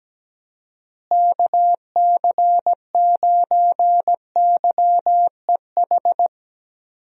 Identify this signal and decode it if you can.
Morse code, heard as a keyed tone, about 17 wpm, 700 Hz. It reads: KC9YEH
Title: USA Call Signs - 6 Characters - Repeated 2x - 17wpm